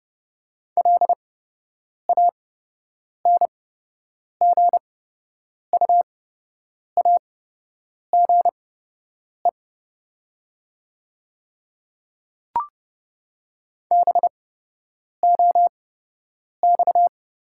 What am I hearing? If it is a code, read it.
Morse code, heard as a keyed tone, about 30 wpm, spaced as for 8 wpm, 700 Hz. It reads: LANGUAGE BOX
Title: 8x Character Spacing - Top 500 Words - 30wpm